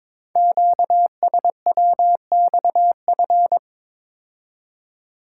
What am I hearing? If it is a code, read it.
Morse code, heard as a keyed tone, about 22 wpm, 700 Hz. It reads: QSWXF
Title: Code Groups - 5 Letters - 22wpm